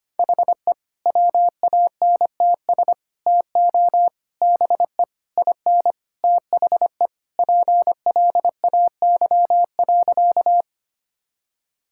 Morse code, heard as a keyed tone, about 25 wpm, 700 Hz. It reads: HE WANTS TO BE IN THE PLAY.
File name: practice / 25wpm / Sentences with 4-Word Phrases - 25wpm